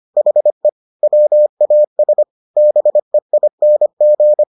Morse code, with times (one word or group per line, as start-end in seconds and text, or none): 0.2-0.7 HE
1.0-2.2 WAS
2.6-4.4 BEING